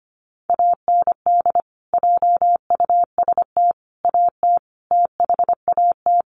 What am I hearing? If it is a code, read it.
Morse code, heard as a keyed tone, about 25 wpm, 700 Hz. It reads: AND JUST AT THAT
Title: Sentences from Top 1500 Words 25wpm